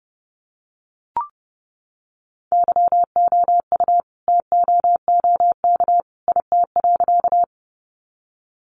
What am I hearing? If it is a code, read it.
Morse code, heard as a keyed tone, about 30 wpm, 700 Hz. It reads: YOU TOOK IT.
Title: Sentences from Top 300 Words 30wpm